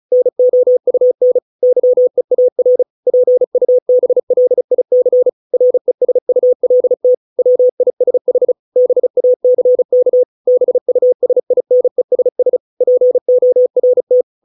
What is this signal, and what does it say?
Morse code, heard as a keyed tone, about 34 wpm, 500 Hz. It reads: NOUN YEAR PUBLIC RESULT WISH BACK BUSINESS PORT